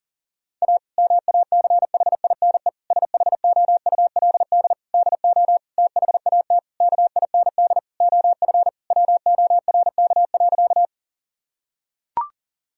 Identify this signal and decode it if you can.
Morse code, heard as a keyed tone, about 37 wpm, 700 Hz. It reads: A MACHINE SHOULD DO THAT KIND OF WORK.
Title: Sentences from Top 1000 Words - Review Part 2 - 40wpm